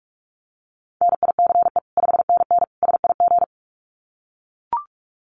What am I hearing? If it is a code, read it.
Morse code, heard as a keyed tone, about 44 wpm, 700 Hz. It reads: NICE 5NN SIG
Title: Common QSO Phrases 45wpm